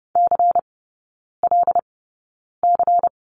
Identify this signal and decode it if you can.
Morse code, heard as a keyed tone, about 30 wpm, spaced as for 9 wpm, 690 Hz. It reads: CLC
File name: Lesson 16e - Letter c - Three Characters 30wpm